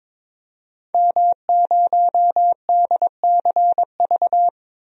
Morse code, heard as a keyed tone, about 22 wpm, 700 Hz. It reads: M0DCV
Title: DXCC-100 Call Signs - 5 Characters - 22wpm